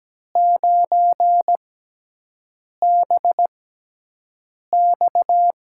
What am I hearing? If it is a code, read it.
Morse code, heard as a keyed tone, about 17 wpm, spaced as for 6 wpm, 700 Hz. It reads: 9BX
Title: USA Call Signs - 6 Characters - 6x Character Spacing - 17wpm